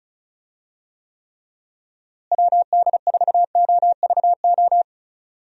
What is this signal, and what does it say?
Morse code, heard as a keyed tone, about 36 wpm, 700 Hz. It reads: WD4OVO
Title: USA Call Signs - 6 Characters - Repeated 2x - 35wpm